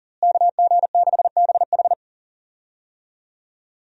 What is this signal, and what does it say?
Morse code, heard as a keyed tone, about 37 wpm, 700 Hz. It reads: KG6BH
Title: USA Call Signs - 5 Characters - Repeated 2x - 40wpm